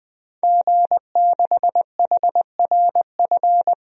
Morse code, heard as a keyed tone, about 20 wpm, 700 Hz. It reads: G6HRF
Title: DXCC-100 Call Signs - 5 Characters - Repeated 2x - 20wpm